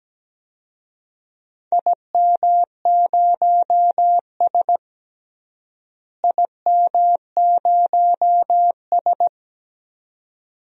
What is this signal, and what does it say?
Morse code, heard as a keyed tone, about 17 wpm, 700 Hz. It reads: IM0S IM0S
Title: DXCC-100 Call Signs - 4 Characters - Repeated 2x - 17wpm